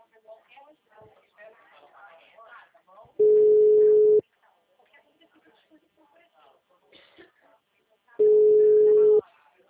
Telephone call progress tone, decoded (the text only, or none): ringback tone